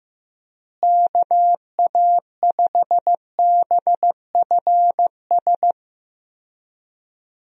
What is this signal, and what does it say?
Morse code, heard as a keyed tone, about 15 wpm, 700 Hz. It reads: KA5BFS
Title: USA Call Signs - 6 Characters - Repeated 2x - 15wpm